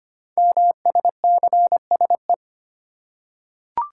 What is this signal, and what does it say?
Morse code, heard as a keyed tone, about 25 wpm, 700 Hz. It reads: MSCSE
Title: Code Groups - 5 Letter:Numbers - 25wpm